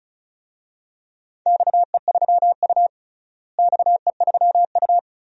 Morse code, heard as a keyed tone, about 36 wpm, 700 Hz. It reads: XE3U XE3U
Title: DXCC-100 Call Signs - 4 Characters - Repeated 2x - 35wpm